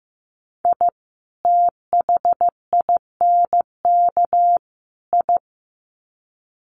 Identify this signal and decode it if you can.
Morse code, heard as a keyed tone, about 15 wpm, 700 Hz. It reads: I THINK I
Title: Sets of 3 Words 15wpm